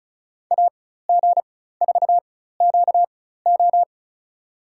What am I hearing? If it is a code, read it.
Morse code, heard as a keyed tone, about 34 wpm, spaced as for 16 wpm, 700 Hz. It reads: AG4QO
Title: USA Call Signs - 5 Characters - 4x Character Spacing - 35wpm